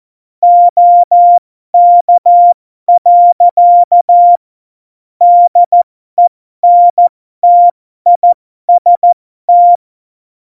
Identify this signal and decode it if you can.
Morse code, heard as a keyed tone, about 14 wpm, spaced as for 12 wpm, 700 Hz. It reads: OK. DENTIST
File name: Morsekurs / Witze